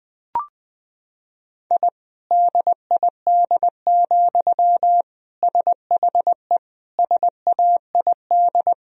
Morse code, heard as a keyed tone, about 21 wpm, 700 Hz. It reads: I DID, SHE SAID
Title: Sentences from Top 1000 Words - Review Part 4 - 20wpm